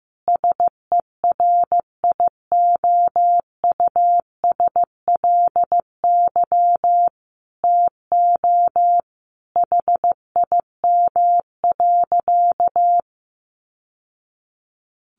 Morse code, heard as a keyed tone, about 15 wpm, 700 Hz. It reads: SERIOUSLY TO HIM.
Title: Sentences from Top 1200 Words 15wpm